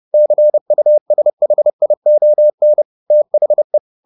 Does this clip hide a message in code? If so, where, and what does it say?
Yes: Morse code from 0.1 to 3.8 s, heard as a keyed tone, about 30 wpm, 600 Hz. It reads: CUSHION THE